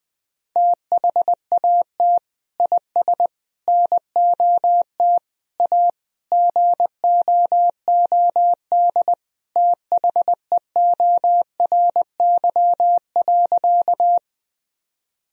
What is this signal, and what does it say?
Morse code, heard as a keyed tone, about 20 wpm, 700 Hz. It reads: THAT IS NOT A GOOD THEORY.
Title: Sentences from Top 1000 Words - Review Part 3 - 20wpm